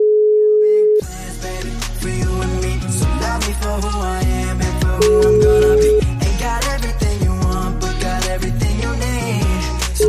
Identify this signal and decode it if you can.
Ringback tone (Europe, 425 Hz), with noise.